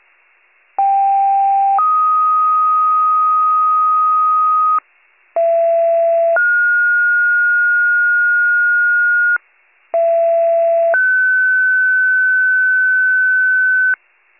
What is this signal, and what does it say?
Two-tone page, heard as a one-second tone, then a three-second tone, about 790 then 1250 Hz, about 670 then 1430 Hz, about 670 then 1500 Hz, with noise. It is heard from 0.8 to 13.9 s.